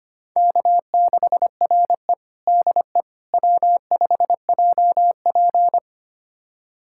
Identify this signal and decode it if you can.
Morse code, heard as a keyed tone, about 25 wpm, 700 Hz. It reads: K6RE DE W5JP
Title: USA Callsign DE Callsign - 4 Characters - 25wpm